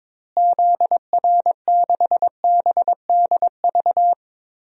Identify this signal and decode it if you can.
Morse code, heard as a keyed tone, about 22 wpm, 700 Hz. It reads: ZR6BDV
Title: DXCC-100 Call Signs - 6 Characters - Repeated 2x - 22wpm